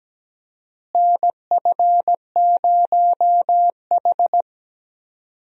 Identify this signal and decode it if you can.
Morse code, heard as a keyed tone, about 17 wpm, 700 Hz. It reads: NF0H
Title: USA Call Signs - 4 Characters - Repeated 2x - 17wpm